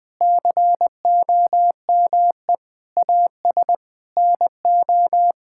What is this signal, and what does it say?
Morse code, heard as a keyed tone, about 20 wpm, 700 Hz. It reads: COME AS NO
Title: Sets of 9 Words - 20wpm